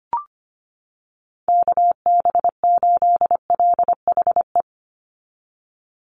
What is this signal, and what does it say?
Morse code, heard as a keyed tone, about 25 wpm, 700 Hz. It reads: KB8LHE
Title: USA Call Signs 25wpm